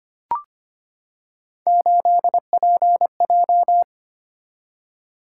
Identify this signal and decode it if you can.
Morse code, heard as a keyed tone, about 25 wpm, 700 Hz. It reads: 8PJ
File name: Three Letter-Number Combinations - 25wpm